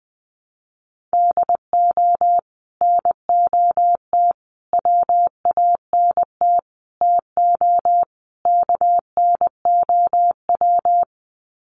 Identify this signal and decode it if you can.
Morse code, heard as a keyed tone, about 20 wpm, 700 Hz. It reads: DO NOT WANT TO KNOW
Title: Sets of 5 Words Encore 20wpm